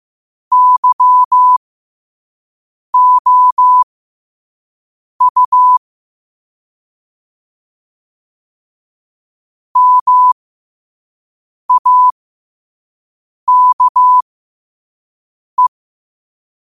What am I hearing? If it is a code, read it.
Morse code, heard as a keyed tone, about 15 wpm, spaced as for 5 wpm, 1000 Hz. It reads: YOU MAKE
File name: qotdfiles / 5wpm